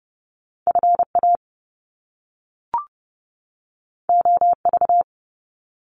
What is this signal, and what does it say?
Morse code, heard as a keyed tone, about 30 wpm, 700 Hz. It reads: FA OV
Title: Top Two Letter Combinations in Words 30wpm